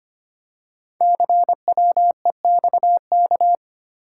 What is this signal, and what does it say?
Morse code, heard as a keyed tone, about 25 wpm, 700 Hz. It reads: CWEXK
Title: Code Groups - 5 Letters - 25wpm